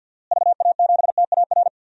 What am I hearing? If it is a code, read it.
Morse code, heard as a keyed tone, about 52 wpm, 700 Hz. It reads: VA7TAR
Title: DXCC-100 Call Signs - 6 Characters - Repeated 3x - 50wpm